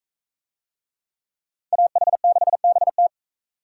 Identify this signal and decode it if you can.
Morse code, heard as a keyed tone, about 41 wpm, 700 Hz. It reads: AH6BT